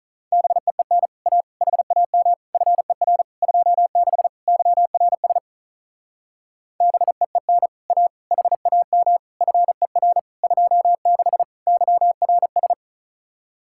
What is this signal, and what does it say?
Morse code, heard as a keyed tone, about 46 wpm, 700 Hz. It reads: BEEN A HAM FER 26 YRS BEEN A HAM FER 26 YRS